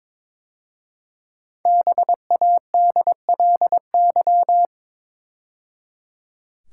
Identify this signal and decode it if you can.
Morse code, heard as a keyed tone, about 22 wpm, 700 Hz. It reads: BADLY
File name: 5 Letter Words - First 3 Letters Given - 22wpm